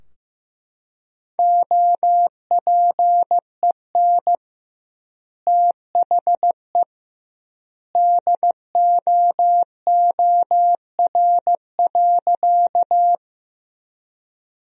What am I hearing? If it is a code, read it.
Morse code, heard as a keyed tone, about 15 wpm, 700 Hz. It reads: OPEN THE DOOR.
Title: 2x Word-Spacing - Sentences from Top 300 Words - 15wpm